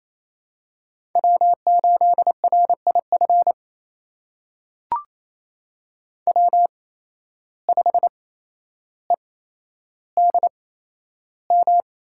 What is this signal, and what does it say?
Morse code, heard as a keyed tone, about 28 wpm, spaced as for 12 wpm, 700 Hz. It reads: W8RIF W5EDM
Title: USA Call Signs - 5 Characters - 8x Character Spacing - 28wpm